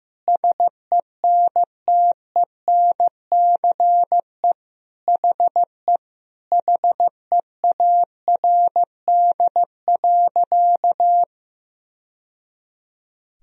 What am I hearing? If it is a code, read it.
Morse code, heard as a keyed tone, about 15 wpm, 700 Hz. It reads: SENTENCE HE HEARD.
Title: Sentences from Top 1000 Words - Review Part 1 - 15wpm